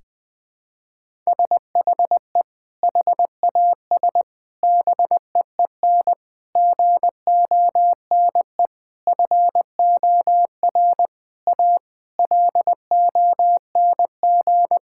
Morse code, heard as a keyed tone, about 20 wpm, 700 Hz. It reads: SHE HAS BEEN GONE FOR A LONG
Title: Sentences with 3-Word Phrases - 20wpm